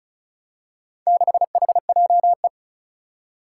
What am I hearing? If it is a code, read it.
Morse code, heard as a keyed tone, about 36 wpm, 700 Hz. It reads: 6HJE
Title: Four Challenging Letter-Number Combinations 35wpm